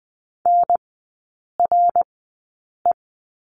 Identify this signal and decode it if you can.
Morse code, heard as a keyed tone, about 20 wpm, spaced as for 8 wpm, 700 Hz. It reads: NRE
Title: 087 Lesson 15e - Number 5 - Three Characters 20wpm